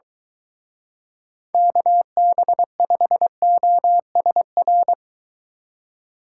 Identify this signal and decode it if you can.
Morse code, heard as a keyed tone, about 23 wpm, 700 Hz. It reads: KB5OSR